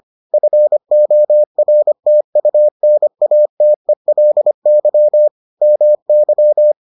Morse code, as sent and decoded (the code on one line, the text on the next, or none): ..-. --- .-. - ..- -. .- - . .-.. -.-- / -- -.--
FORTUNATELY MY